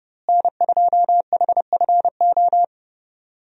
Morse code, heard as a keyed tone, about 30 wpm, 700 Hz. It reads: N2HFO
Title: USA Call Signs - 5 Characters - Repeated 2x - 30wpm